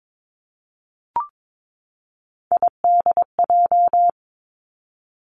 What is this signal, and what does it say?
Morse code, heard as a keyed tone, about 22 wpm, 700 Hz. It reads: IDJ
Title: Three Challenging Letter-Number Combinations 22wpm